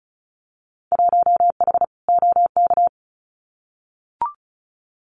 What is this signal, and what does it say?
Morse code, heard as a keyed tone, about 34 wpm, 700 Hz. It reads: 1H OK
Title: Winter Field Day Exchanges 35wpm